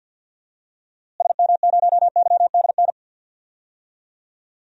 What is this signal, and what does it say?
Morse code, heard as a keyed tone, about 52 wpm, 700 Hz. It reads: SM0YDN
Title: DXCC-100 Call Signs - 6 Characters - Repeated 2x - 50wpm